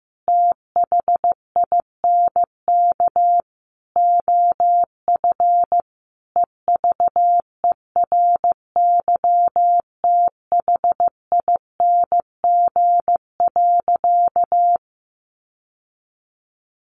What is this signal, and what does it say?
Morse code, heard as a keyed tone, about 15 wpm, 700 Hz. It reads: THINK OF EVERYTHING.
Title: Sentences from Top 1000 Words 15wpm